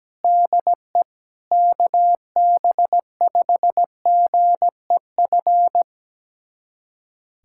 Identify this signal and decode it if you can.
Morse code, heard as a keyed tone, about 17 wpm, 700 Hz. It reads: DE KB5GEF
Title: USA Callsign DE Callsign - 6 Characters - 17wpm